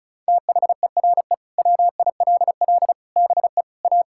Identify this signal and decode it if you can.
Morse code, heard as a keyed tone, about 36 wpm, 700 Hz. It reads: THERE WILL BE A
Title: Sentences from Top 1600 Words 35wpm